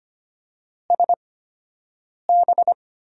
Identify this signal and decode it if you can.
Morse code, heard as a keyed tone, about 25 wpm, spaced as for 7 wpm, 700 Hz. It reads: SB
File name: Top 90 QSO Elements - 8x Character Spacing - 25wpm